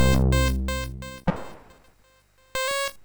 There is no code.